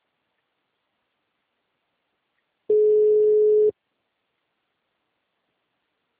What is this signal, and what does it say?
Ringback tone (Europe, 425 Hz).
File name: out- -2011- - - .2756